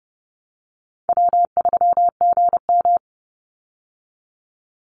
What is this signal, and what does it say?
Morse code, heard as a keyed tone, about 30 wpm, 700 Hz. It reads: W3GM
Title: USA Call Signs 30wpm